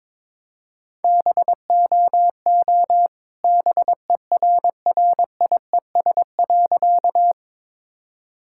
Morse code, heard as a keyed tone, about 21 wpm, 690 Hz. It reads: BOO BERRIES.
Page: morse-code-ninja-practice-22wpm